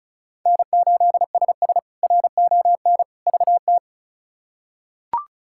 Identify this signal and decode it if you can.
Morse code, heard as a keyed tone, about 36 wpm, 700 Hz. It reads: N8SS RON VT
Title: K1USN SST QSO Practice - 35wpm